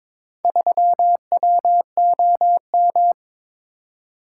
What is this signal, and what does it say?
Morse code, heard as a keyed tone, about 22 wpm, 700 Hz. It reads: 3WOM
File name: practice / 22wpm / Four Letter-Number Combinations - 22wpm